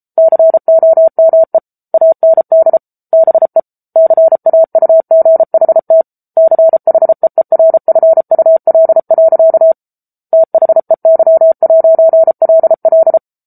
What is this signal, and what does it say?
Morse code, heard as a keyed tone, about 34 wpm, 650 Hz. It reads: COME AND BE CAUGHT CHEERFUL. THEY'LL
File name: Book Two / 07-Chapter 7 - 33 WPM